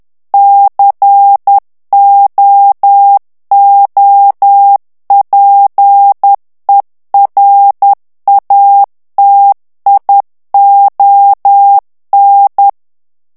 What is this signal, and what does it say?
Morse code, heard as a keyed tone, about 10 wpm, 800 Hz. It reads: COOPERATION
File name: ANGLAIS texte clair 3 vitesse 10 mots minute